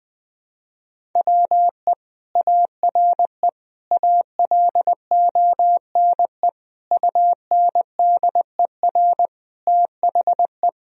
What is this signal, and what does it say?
Morse code, heard as a keyed tone, about 21 wpm, 700 Hz. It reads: WE ARE ALONE UNDER THE
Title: Sentences from Top 1000 Words - Review Part 1 - 20wpm